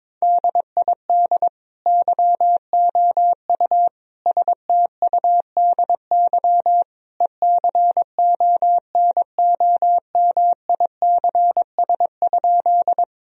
Morse code, heard as a keyed tone, about 22 wpm, 700 Hz. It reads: DID YOU STUDY ECONOMICS?
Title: Sentences from Top 1000 Words - Review Part 3 - 22wpm